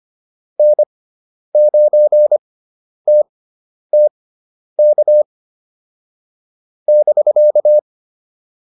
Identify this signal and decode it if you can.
Morse code, heard as a keyed tone, about 25 wpm, spaced as for 10 wpm, 600 Hz. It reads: N9TTK <BK>